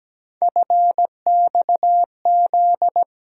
Morse code, heard as a keyed tone, about 17 wpm, 700 Hz. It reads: FXZ
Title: USA Call Sign Postfixes - 3 Letters 17wpm